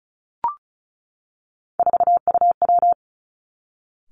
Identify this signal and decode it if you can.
Morse code, heard as a keyed tone, about 36 wpm, 700 Hz. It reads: 4UW
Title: Three Challenging Letter-Number Combinations 35wpm